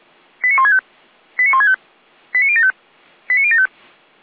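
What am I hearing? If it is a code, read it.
Five-tone selective call (ZVEI), heard as quick runs of five tones, tones of about 70 ms. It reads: 89165 89165 89075 89075
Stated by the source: ZVEI five-tone call 89174 received on SDR.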